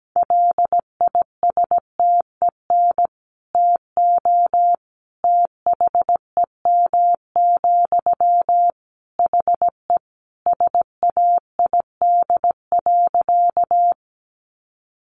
Morse code, heard as a keyed tone, about 17 wpm, 700 Hz. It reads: LISTEN TO THEM, HE SAID.
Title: Sentences from Top 400 Words 17wpm